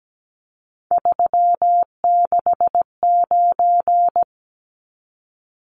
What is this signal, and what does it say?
Morse code, heard as a keyed tone, about 17 wpm, 700 Hz. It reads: 369
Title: Numbers - 3-Digit 17wpm